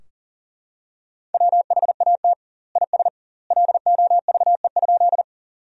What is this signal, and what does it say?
Morse code, heard as a keyed tone, about 40 wpm, 700 Hz. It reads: WHAT IS LOVE?
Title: 2x Word-Spacing - Sentences from Top 400 Words - 40wpm